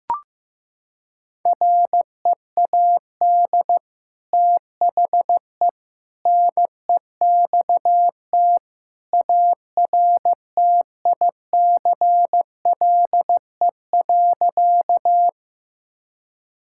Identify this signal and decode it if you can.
Morse code, heard as a keyed tone, about 15 wpm, 700 Hz. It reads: READ THE NEXT ARTICLE.
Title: Sentences from Top 1000 Words - Review Part 5 - 15wpm